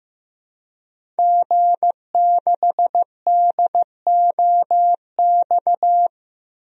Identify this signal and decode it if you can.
Morse code, heard as a keyed tone, about 14 wpm, 700 Hz. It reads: G6DOX
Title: DXCC-100 Call Signs - 5 Characters - Repeated 3x - 15wpm